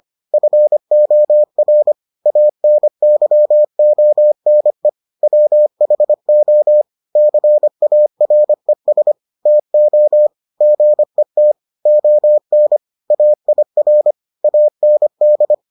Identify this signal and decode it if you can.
Morse code, heard as a keyed tone, about 25 wpm, 600 Hz. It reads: FOR ANYONE WHO CARES TO GET ON AIR AND